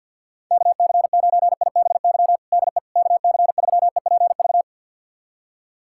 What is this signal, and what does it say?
Morse code, heard as a keyed tone, about 51 wpm, 700 Hz. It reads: KC9IBY DE KC3EPV
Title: USA Callsign DE Callsign - 6 Characters - 50wpm